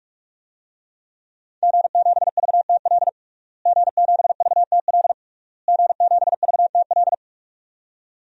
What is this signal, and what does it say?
Morse code, heard as a keyed tone, about 46 wpm, 700 Hz. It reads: G7VTL G7VTL G7VTL
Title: DXCC-100 Call Signs - 5 Characters - Repeated 3x - 45wpm